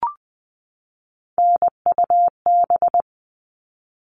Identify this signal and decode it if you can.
Morse code, heard as a keyed tone, about 20 wpm, 700 Hz. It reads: NUB
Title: All Cut Numbers - 3-Digit 20wpm